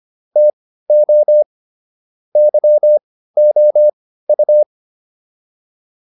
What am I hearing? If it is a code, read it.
Morse code, heard as a keyed tone, about 25 wpm, spaced as for 15 wpm, 600 Hz. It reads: TO YOU